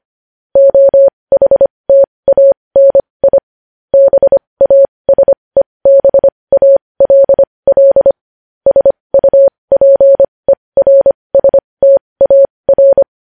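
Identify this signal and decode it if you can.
Morse code, heard as a keyed tone, about 25 wpm, spaced as for 20 wpm, 550 Hz. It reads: OHTANI BASEBALL SUPERSTAR
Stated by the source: Morse code news headlines sent at 20 wpm.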